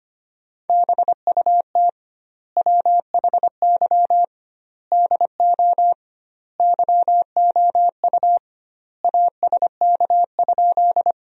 Morse code, heard as a keyed tone, about 25 wpm, 700 Hz. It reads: BUT WHY DO YOU ASK?
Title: 2x Word-Spacing - Sentences from Top 200 Words - 25wpm